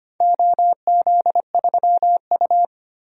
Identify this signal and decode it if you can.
Morse code, heard as a keyed tone, about 25 wpm, 700 Hz. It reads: OZ3U
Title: DXCC-100 Call Signs - 4 Characters - 25wpm